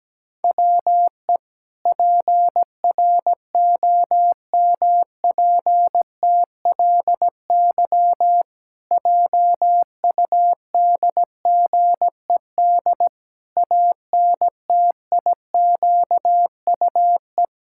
Morse code, heard as a keyed tone, about 17 wpm, 700 Hz. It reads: WE PROMPTLY JUDGED ANTIQUE
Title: Pangram Challenge v2 - 17wpm